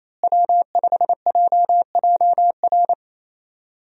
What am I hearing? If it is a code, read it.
Morse code, heard as a keyed tone, about 28 wpm, 700 Hz. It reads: W5JJR
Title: USA Call Signs - 5 Characters - Repeated 3x - 28wpm